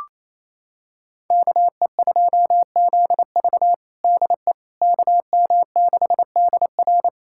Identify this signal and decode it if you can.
Morse code, heard as a keyed tone, about 28 wpm, 700 Hz. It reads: KE2ZV DE KM6DR
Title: USA Callsign DE Callsign - 5 Characters - 28wpm